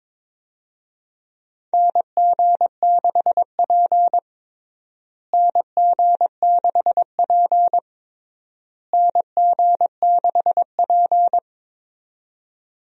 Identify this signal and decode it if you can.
Morse code, heard as a keyed tone, about 21 wpm, 700 Hz. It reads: NG6P NG6P NG6P
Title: USA Call Signs - 4 Characters - Repeated 3x - 22wpm